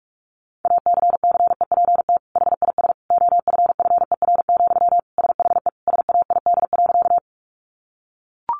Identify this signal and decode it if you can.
Morse code, heard as a keyed tone, about 44 wpm, 700 Hz. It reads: ACCEPT HIS OFFER, SHE SAID.
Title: Sentences from Top 1300 Words 45wpm